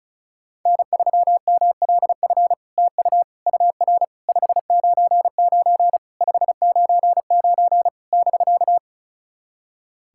Morse code, heard as a keyed tone, about 36 wpm, 700 Hz. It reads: N3MLF TU UR 599 599 <BK>
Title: SOTA QSO Practice 35wpm